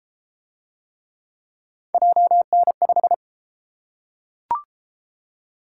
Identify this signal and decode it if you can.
Morse code, heard as a keyed tone, about 32 wpm, 700 Hz. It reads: JN5